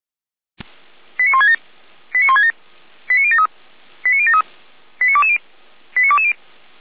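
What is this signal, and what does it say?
Five-tone selective call (ZVEI), heard as quick runs of five tones, tones of about 80 ms. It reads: 89167 89167 89073 89073 89220 89220
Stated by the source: ZVEI five-tone selective call received on SDR.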